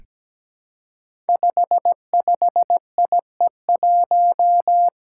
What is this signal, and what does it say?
Morse code, heard as a keyed tone, about 17 wpm, 700 Hz. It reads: 55IE1
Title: Five Challenging Letter-Number Combinations 17wpm